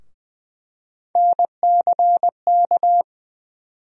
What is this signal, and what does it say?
Morse code, heard as a keyed tone, about 20 wpm, 700 Hz. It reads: NCK
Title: USA Call Sign Postfixes - 3 Letters 20wpm